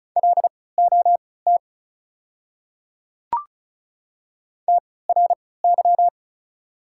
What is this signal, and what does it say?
Morse code, heard as a keyed tone, about 34 wpm, spaced as for 20 wpm, 690 Hz. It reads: LOT TRY
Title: 3x Character Spacing - Top 500 Words - 35wpm